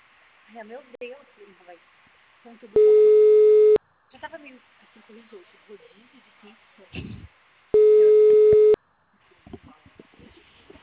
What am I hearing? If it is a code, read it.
Ringback tone (Europe, 425 Hz), with speech.
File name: out- -2005- - - .167